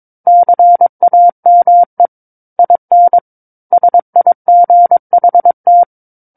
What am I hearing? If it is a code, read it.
Morse code, heard as a keyed tone, about 22 wpm, 700 Hz. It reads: CAME IN SIGHT
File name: 06-Chapter-06 22 WPM